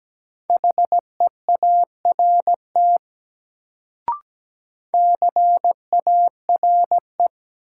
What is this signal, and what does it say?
Morse code, heard as a keyed tone, about 17 wpm, 700 Hz. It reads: HEART CARE
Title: Sets of 3 Words - New Word Drill - 17wpm